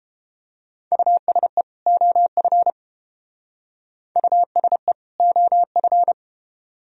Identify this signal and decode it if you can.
Morse code, heard as a keyed tone, about 29 wpm, 700 Hz. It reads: USE OF USE OF